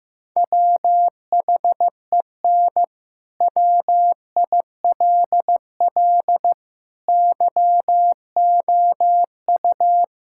Morse code, heard as a keyed tone, about 15 wpm, 700 Hz. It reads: WHEN WILL YOU
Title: Sentences from Top 1000 Words - Review Part 6 - 15wpm